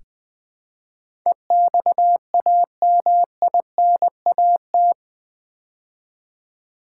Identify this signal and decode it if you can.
Morse code, heard as a keyed tone, about 20 wpm, 700 Hz. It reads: EXAMINAT
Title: Build A Word - 4 Letters At A Time - 11-Letter Words - 20wpm